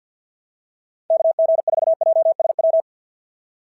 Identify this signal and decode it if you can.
Morse code, heard as a keyed tone, about 47 wpm, 690 Hz. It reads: KG4JSW
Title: SOTA QSO Practice 50wpm